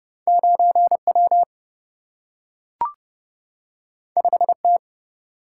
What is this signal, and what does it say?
Morse code, heard as a keyed tone, about 30 wpm, 700 Hz. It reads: 9W 5T
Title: Single-Letter Single-Number 30wpm